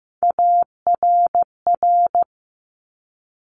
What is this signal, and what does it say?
Morse code, heard as a keyed tone, about 15 wpm, 700 Hz. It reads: ARR
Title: Top Three Letter Combinations in Words 15wpm